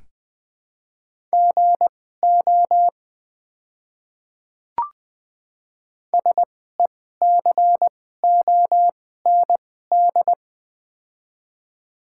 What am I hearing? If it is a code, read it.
Morse code, heard as a keyed tone, about 20 wpm, spaced as for 14 wpm, 700 Hz. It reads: GO SECOND
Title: 2x Character Spacing - Top 300 Words - 20wpm